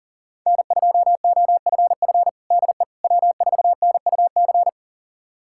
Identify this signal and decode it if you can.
Morse code, heard as a keyed tone, about 40 wpm, 690 Hz. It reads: N2OFF DE W4NUC
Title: USA Callsign DE Callsign - 5 Characters - 40wpm